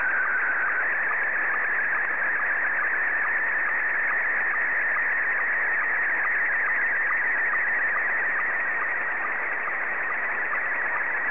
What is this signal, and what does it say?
SSTV (slow-scan television), Scottie 1.